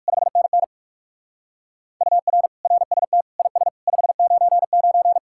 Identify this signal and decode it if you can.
Morse code, heard as a keyed tone, about 55 wpm, 700 Hz. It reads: 5NN UR RST IS 599
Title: Speed-Racing Common QSO Phrases Encore - 45wpm